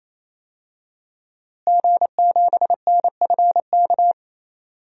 Morse code, heard as a keyed tone, about 28 wpm, 700 Hz. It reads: G7NFK
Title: DXCC-100 Call Signs - 5 Characters - Repeated 3x - 28wpm